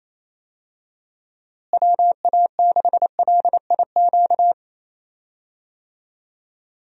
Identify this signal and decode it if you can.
Morse code, heard as a keyed tone, about 28 wpm, 700 Hz. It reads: WA6LIQ
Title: USA Call Signs - 6 Characters - Repeated 2x - 28wpm